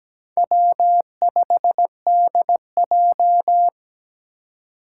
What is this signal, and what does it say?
Morse code, heard as a keyed tone, about 17 wpm, 700 Hz. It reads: W5DJ
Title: Four Challenging Letter-Number Combinations 17wpm